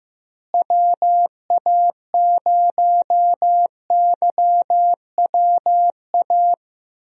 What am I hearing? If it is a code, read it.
Morse code, heard as a keyed tone, about 15 wpm, 700 Hz. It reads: WA0YWA
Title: USA Call Signs - 6 Characters - Repeated 2x - 15wpm